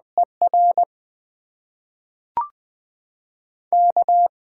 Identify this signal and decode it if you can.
Morse code, heard as a keyed tone, about 20 wpm, spaced as for 5 wpm, 700 Hz. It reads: ERK
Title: USA Call Signs - 4 Characters - 8x Character Spacing - 20wpm